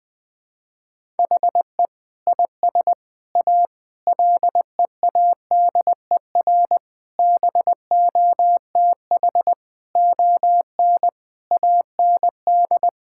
Morse code, heard as a keyed tone, about 19 wpm, 700 Hz. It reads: HE IS A LEADER BOTH ON AND
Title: Sentences with 6-Word Phrases - Spoken Beginning - 20wpm